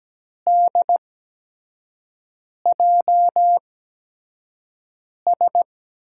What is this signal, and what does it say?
Morse code, heard as a keyed tone, about 17 wpm, spaced as for 5 wpm, 700 Hz. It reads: DJS
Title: USA Call Signs - 6 Characters - 8x Character Spacing - 17wpm